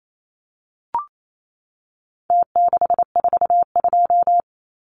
Morse code, heard as a keyed tone, about 28 wpm, 700 Hz. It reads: T642
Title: Numbers - Contest Style - 4-Digit 28wpm